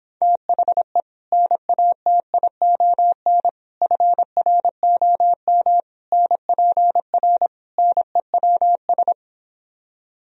Morse code, heard as a keyed tone, about 26 wpm, 700 Hz. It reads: THE NATION FROM NPR NEWS
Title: Speed-Racing - Sets of 5 Words 22wpm